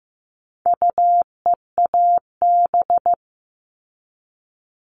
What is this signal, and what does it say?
Morse code, heard as a keyed tone, about 15 wpm, 700 Hz. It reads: UEAB
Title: All Cut Numbers - 4-Digit 15wpm